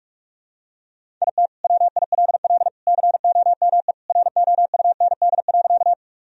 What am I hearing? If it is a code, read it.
Morse code, heard as a keyed tone, about 46 wpm, 700 Hz. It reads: IT WILL COME ROUND.
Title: Sentences from Top 1000 Words - Review Part 1 - 45wpm